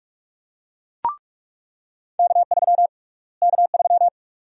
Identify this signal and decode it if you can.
Morse code, heard as a keyed tone, about 43 wpm, 700 Hz. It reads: K3 K3
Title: USA Call Sign Prefixes - Repeated 2x - 45wpm